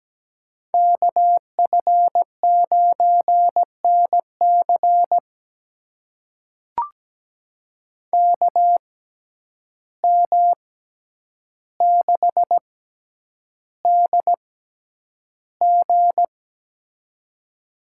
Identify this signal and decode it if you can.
Morse code, heard as a keyed tone, about 17 wpm, spaced as for 9 wpm, 700 Hz. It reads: KF9NC KM6DG
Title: USA Call Signs - 5 Characters - 6x Character Spacing - 17wpm